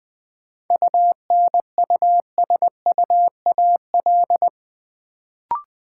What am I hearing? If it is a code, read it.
Morse code, heard as a keyed tone, about 20 wpm, 700 Hz. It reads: UNUSUAL
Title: Sentences from Top 1400 Words - New Word Drill - 20wpm